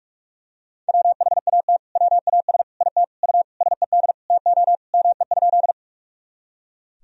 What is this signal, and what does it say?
Morse code, heard as a keyed tone, about 43 wpm, 700 Hz. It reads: WHAT WAS IT U SED TO ME?
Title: Sentences from Top 200 Words and Abbreviations 45wpm